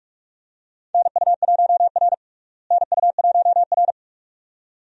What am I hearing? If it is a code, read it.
Morse code, heard as a keyed tone, about 42 wpm, 700 Hz. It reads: NU1R NU1R
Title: USA Call Signs - 4 Characters - Repeated 2x - 45wpm